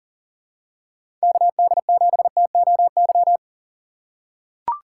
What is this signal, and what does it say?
Morse code, heard as a keyed tone, about 40 wpm, 700 Hz. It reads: KD7TOY
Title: USA Call Signs - 4x Character Spacing - 40wpm